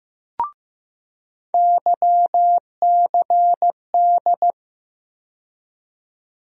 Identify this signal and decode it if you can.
Morse code, heard as a keyed tone, about 15 wpm, 700 Hz. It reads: YCD
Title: USA Call Sign Postfixes - 3 Letters 15wpm